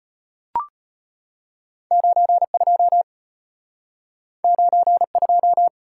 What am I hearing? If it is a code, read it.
Morse code, heard as a keyed tone, about 34 wpm, 700 Hz. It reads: 92 92